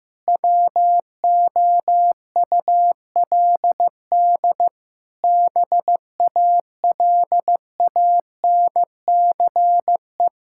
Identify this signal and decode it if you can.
Morse code, heard as a keyed tone, about 15 wpm, 700 Hz. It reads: WOULD BALANCE
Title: Sentences from Top 1000 Words - Review Part 6 - 15wpm